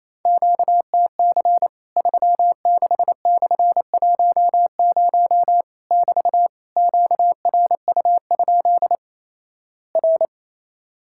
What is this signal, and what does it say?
Morse code, heard as a keyed tone, about 28 wpm, 700 Hz. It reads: QTC 36/10 <BT> QRU? R
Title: Worked All Europe DX Contest - QTC Exchange - 28wpm